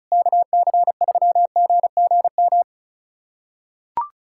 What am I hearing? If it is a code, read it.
Morse code, heard as a keyed tone, about 36 wpm, 700 Hz. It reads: KC3GGM
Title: USA Callsign DE Callsign - 6 Characters - 35wpm